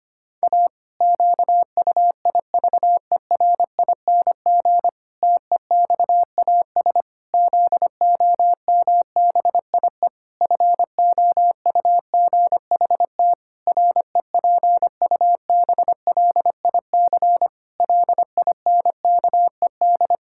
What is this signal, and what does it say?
Morse code, heard as a keyed tone, about 25 wpm, 700 Hz. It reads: A QUIVERING TEXAS ZOMBIE FOUGHT REPUBLIC LINKED
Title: Pangram Challenge v2 - 25wpm